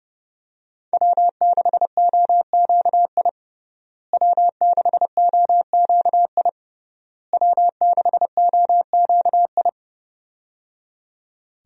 Morse code, heard as a keyed tone, about 30 wpm, 700 Hz. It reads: W6OQI W6OQI W6OQI
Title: USA Call Signs - 5 Characters - Repeated 3x - 30wpm